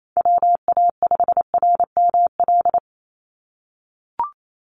Morse code, heard as a keyed tone, about 28 wpm, 700 Hz. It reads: WA5RML
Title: Call Signs - 28wpm